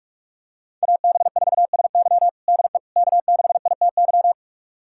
Morse code, heard as a keyed tone, about 47 wpm, 700 Hz. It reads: AB4SY DE K6ITY
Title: USA Callsign DE Callsign - 5 Characters - 45wpm